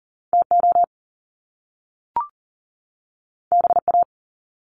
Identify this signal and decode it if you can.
Morse code, heard as a keyed tone, about 40 wpm, 700 Hz. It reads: TO BA
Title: Top Two Letter Combinations in Words 40wpm